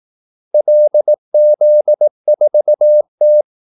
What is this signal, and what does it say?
Morse code, heard as a keyed tone, about 18 wpm, 600 Hz. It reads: LZ4T
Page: Kaldesignaler 18 wpm